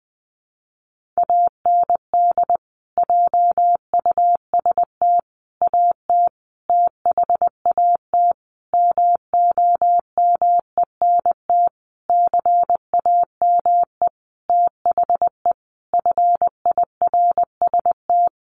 Morse code, heard as a keyed tone, about 20 wpm, 700 Hz. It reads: AND JUST AT THAT MOMENT CAME THE FIRST
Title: Sentences from Top 1500 Words 20wpm